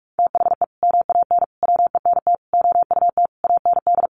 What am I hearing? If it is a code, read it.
Morse code, heard as a keyed tone, about 44 wpm, 700 Hz. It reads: THE MAN WENT OUT AND
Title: Sentences from Top 1400 Words 45wpm